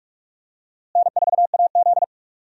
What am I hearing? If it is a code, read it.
Morse code, heard as a keyed tone, about 44 wpm, 700 Hz. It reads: N4AZ
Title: USA Call Signs - 4 Characters - Repeated 2x - 45wpm